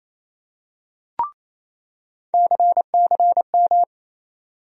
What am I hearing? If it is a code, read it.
Morse code, heard as a keyed tone, about 28 wpm, 700 Hz. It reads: CCM
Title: USA Call Sign Postfixes - 3 Letters 28wpm